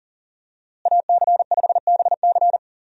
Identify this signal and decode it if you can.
Morse code, heard as a keyed tone, about 37 wpm, 700 Hz. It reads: AC5BC
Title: USA Call Signs - 5 Characters - Repeated 2x - 40wpm